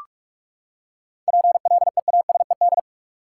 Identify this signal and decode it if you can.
Morse code, heard as a keyed tone, about 46 wpm, 700 Hz. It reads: PLEASED
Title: Seven Letter Words - 45wpm